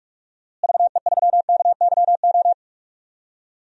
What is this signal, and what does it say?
Morse code, heard as a keyed tone, about 46 wpm, 700 Hz. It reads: VE3KYO
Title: DXCC-100 Call Signs - 6 Characters - Repeated 2x - 45wpm